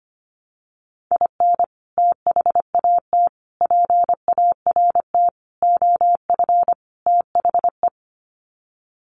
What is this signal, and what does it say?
Morse code, heard as a keyed tone, about 25 wpm, 700 Hz. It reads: IN THAT PART OF THE
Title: Sets of 5 Words Encore2 25wpm